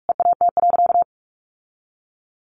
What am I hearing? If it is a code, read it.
Morse code, heard as a keyed tone, about 44 wpm, 700 Hz. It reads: EAT.
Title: Sentences from Top 1300 Words 45wpm